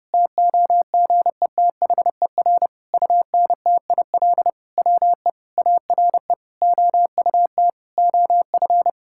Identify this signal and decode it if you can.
Morse code, heard as a keyed tone, about 30 wpm, 700 Hz. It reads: TOGETHER UNTIL WE ARE OUT OF
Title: Sentences from Top 1600 Words 30wpm